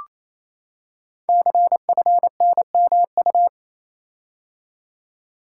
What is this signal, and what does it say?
Morse code, heard as a keyed tone, about 29 wpm, 700 Hz. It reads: CFNMU
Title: Code Groups - 5 Letters - 28wpm